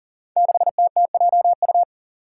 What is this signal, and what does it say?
Morse code, heard as a keyed tone, about 39 wpm, 700 Hz. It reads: 6TTJU
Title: Five Challenging Letter-Number Combinations 40wpm